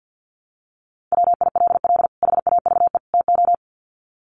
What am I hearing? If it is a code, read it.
Morse code, heard as a keyed tone, about 50 wpm, 700 Hz. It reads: WILL HAVE TO
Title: Sets of 3 Words 50wpm